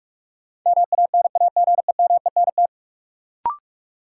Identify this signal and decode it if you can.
Morse code, heard as a keyed tone, about 46 wpm, 700 Hz. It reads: MANAGEMENT
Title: Sentences from Top 1200 Words - New Word Drill Alt - 45wpm